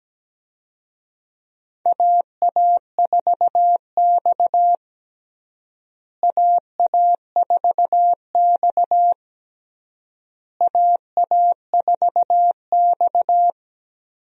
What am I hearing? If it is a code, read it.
Morse code, heard as a keyed tone, about 17 wpm, 700 Hz. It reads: AA4X AA4X AA4X
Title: USA Call Signs - 4 Characters - Repeated 3x - 17wpm